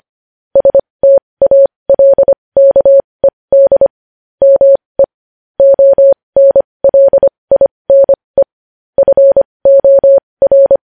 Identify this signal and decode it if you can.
Morse code, heard as a keyed tone, about 25 wpm, spaced as for 20 wpm, 550 Hz. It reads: STALKED ME ONLINE FOR
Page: news-headlines-in-morse-code-at-20-wpm